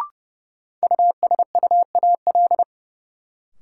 Morse code, heard as a keyed tone, about 32 wpm, 700 Hz. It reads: USUAL
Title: Sentences from Top 400 Words - New Word Drill - 30wpm